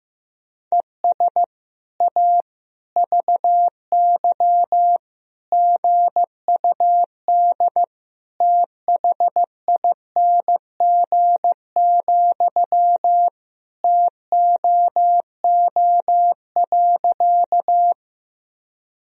Morse code, heard as a keyed tone, about 15 wpm, 700 Hz. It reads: ES A VY GUD THING, TOO.
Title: Sentences from Top 200 Words and Abbreviations 15wpm